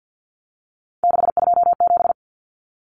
Morse code, heard as a keyed tone, about 50 wpm, 700 Hz. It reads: 627
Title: Common Cut Numbers - 3-Digit 50wpm